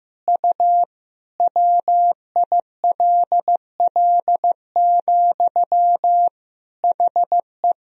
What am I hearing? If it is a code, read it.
Morse code, heard as a keyed tone, about 15 wpm, 700 Hz. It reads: U WILL, HE
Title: Sentences from Top 200 Words and Abbreviations 15wpm